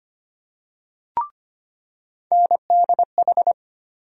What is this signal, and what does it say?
Morse code, heard as a keyed tone, about 25 wpm, 700 Hz. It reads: NDH
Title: USA Call Sign Postfixes - 3 Letters 25wpm